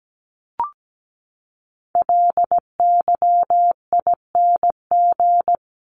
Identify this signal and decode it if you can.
Morse code, heard as a keyed tone, about 17 wpm, 700 Hz. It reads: LYING
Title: ING Suffix - Top 100 Words with Base Words 17wpm